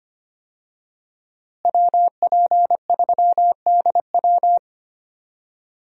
Morse code, heard as a keyed tone, about 25 wpm, 700 Hz. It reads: WP3DW
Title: USA Call Signs - 5 Characters - Repeated 2x - 25wpm